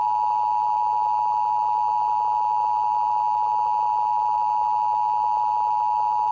Emergency Alert System (SAME) tones (the 853 and 960 Hz attention tone). It is heard from 0.7 to 5.7 s.